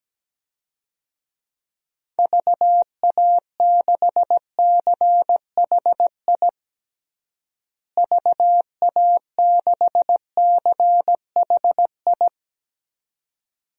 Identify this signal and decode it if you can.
Morse code, heard as a keyed tone, about 17 wpm, 700 Hz. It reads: VA6CHI VA6CHI
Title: DXCC-100 Call Signs - 6 Characters - Repeated 2x - 17wpm